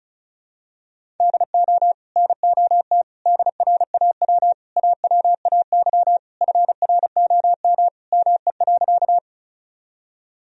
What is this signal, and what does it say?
Morse code, heard as a keyed tone, about 34 wpm, 700 Hz. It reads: DO NOT DRAW AWAY FROM ME.
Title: Sentences from Top 1000 Words - Review Part 1 - 35wpm